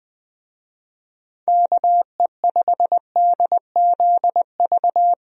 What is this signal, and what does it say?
Morse code, heard as a keyed tone, about 20 wpm, 700 Hz. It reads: KE5DZV
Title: USA Call Signs - 6 Characters - Repeated 2x - 20wpm